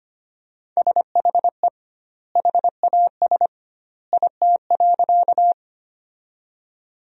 Morse code, heard as a keyed tone, about 25 wpm, 700 Hz. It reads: SHE HAS IT.
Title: 2x Word-Spacing - Sentences from Top 100 Words - 25wpm